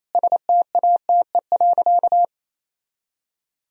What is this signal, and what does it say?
Morse code, heard as a keyed tone, about 28 wpm, 700 Hz. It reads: STATE.
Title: Sentences from Top 1000 Words - Review Part 2 - 28wpm